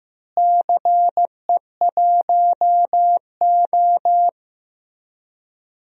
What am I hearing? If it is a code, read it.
Morse code, heard as a keyed tone, about 16 wpm, 700 Hz. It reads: CE1O
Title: DXCC-100 Call Signs - 4 Characters - Repeated 3x - 15wpm